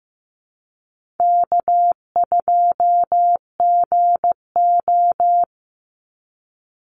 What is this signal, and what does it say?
Morse code, heard as a keyed tone, about 15 wpm, 700 Hz. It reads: K2GO
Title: Easy Call Signs Repeated 3x - 15wpm